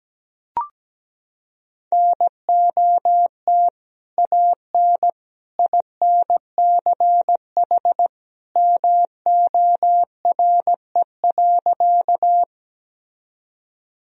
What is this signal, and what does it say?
Morse code, heard as a keyed tone, about 17 wpm, 700 Hz. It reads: NOT AN INCH MORE.
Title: Sentences from Top 1000 Words - Review Part 3 - 17wpm